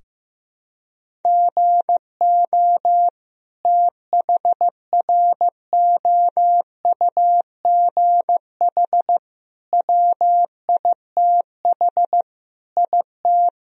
Morse code, heard as a keyed tone, about 15 wpm, 700 Hz. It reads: GO THROUGH WITH IT
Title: Sentences from Top 1000 Words - Review Part 1 - 15wpm